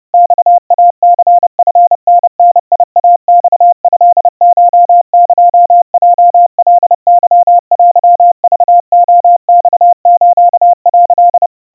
Morse code (Japanese Wabun code), heard as a keyed tone, about 30 wpm, 700 Hz. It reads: マイニチタダイマトコエヲカケテクレマス」
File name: kawa / Morse sample / kamyu 110 cpm 700 Hz Crispy